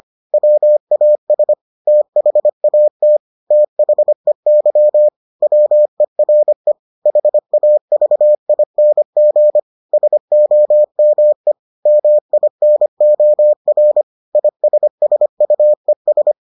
Morse code, heard as a keyed tone, about 25 wpm, 600 Hz. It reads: WAS THAT THEY WERE HAVING SOME MINOR ISSUES